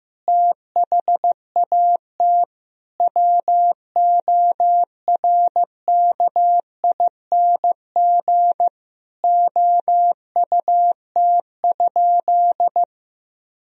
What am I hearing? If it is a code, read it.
Morse code, heard as a keyed tone, about 15 wpm, 700 Hz. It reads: THAT WORKING OUT?
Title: Sentences from Top 1000 Words - Review Part 4 - 15wpm